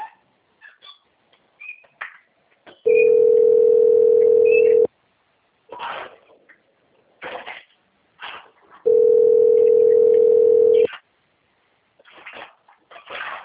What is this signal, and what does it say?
Ringback tone (North America), 440 + 480 Hz.